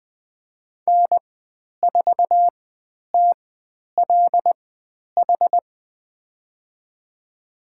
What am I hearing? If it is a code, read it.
Morse code, heard as a keyed tone, about 20 wpm, spaced as for 10 wpm, 700 Hz. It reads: N4TLH
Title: 102 Lesson 17g - Letter u - Call Signs 20wpm